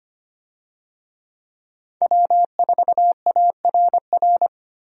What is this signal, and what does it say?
Morse code, heard as a keyed tone, about 25 wpm, 700 Hz. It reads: W4ARR
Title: USA Call Signs - 5 Characters - Repeated 3x - 25wpm